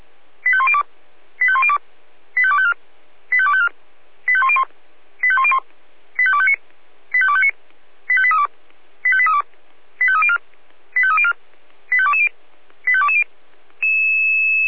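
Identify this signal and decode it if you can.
Five-tone selective call (ZVEI), heard as quick runs of five tones, tones of about 70 ms. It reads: 85102 85102 85254 85254 85101 85101 85259 85259 85832 85832 85304 85304 85220 85220
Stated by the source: ZVEI five-tone selective call 85211 received on SDR.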